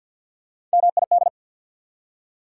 Morse code, heard as a keyed tone, about 48 wpm, 690 Hz. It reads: MID
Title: Prefixes and Suffixes 50wpm